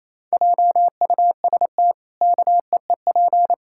Morse code, heard as a keyed tone, about 28 wpm, 700 Hz. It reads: JUST KEEP
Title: Pangram Challenge v2 - 28wpm